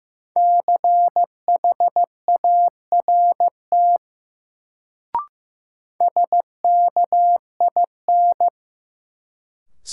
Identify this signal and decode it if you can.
Morse code, heard as a keyed tone, about 15 wpm, 700 Hz. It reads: CHART SKIN
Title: Sentences from Top 1300 Words - New Word Drill - 15wpm